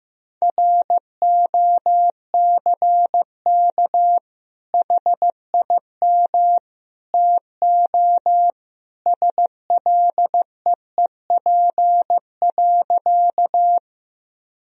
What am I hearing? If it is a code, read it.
Morse code, heard as a keyed tone, about 15 wpm, 700 Hz. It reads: ROCK HIM TO SLEEP.
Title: Sentences from Top 1000 Words - Review Part 2 - 15wpm